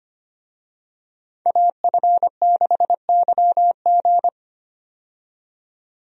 Morse code, heard as a keyed tone, about 25 wpm, 700 Hz. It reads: AF6YG
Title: USA Call Signs - 5 Characters - 25wpm